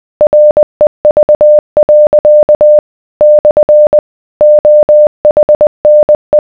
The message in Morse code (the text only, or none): REV. / OHNE